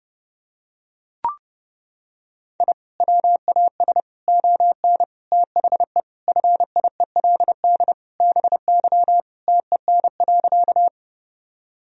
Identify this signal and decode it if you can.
Morse code, heard as a keyed tone, about 32 wpm, 700 Hz. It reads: I WAS ON THE FIELD BY TEN.
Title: Sentences from Top 1000 Words - Review Part 2 - 30wpm